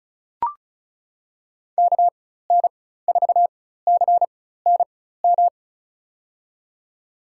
Morse code, heard as a keyed tone, about 36 wpm, spaced as for 16 wpm, 700 Hz. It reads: KN4CNM
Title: USA Call Signs - 6 Characters - 4x Character Spacing - 35wpm